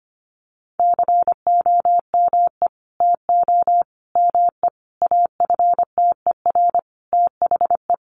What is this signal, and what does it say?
Morse code, heard as a keyed tone, about 25 wpm, 700 Hz. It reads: COME TO ME AFTER THE
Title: Sentences from Top 900 Words 25wpm